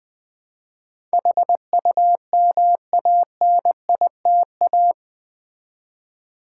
Morse code, heard as a keyed tone, about 20 wpm, 700 Hz. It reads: HUMANITA
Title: Build A Word - 4 Letters At A Time - 12-Letter Words - 20wpm